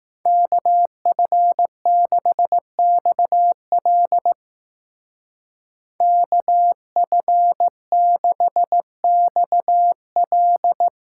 Morse code, heard as a keyed tone, about 17 wpm, 700 Hz. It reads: KF6XL KF6XL
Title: Speed Racing - USA 2x2 Call Signs 15wpm